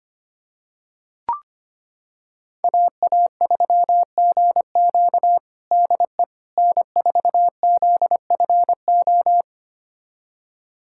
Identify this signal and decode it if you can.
Morse code, heard as a keyed tone, about 25 wpm, 700 Hz. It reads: AA3GQ DE N4ZFO
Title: USA Callsign DE Callsign - 5 Characters - 25wpm